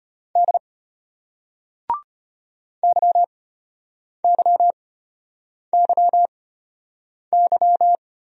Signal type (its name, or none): Morse code